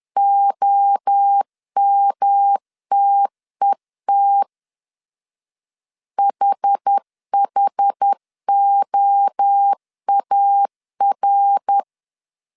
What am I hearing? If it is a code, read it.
Morse code, heard as a keyed tone, about 10 wpm, 800 Hz. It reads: OMTET HHOAR
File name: CW / DET / 1 3 1500C S B 10wpm 3335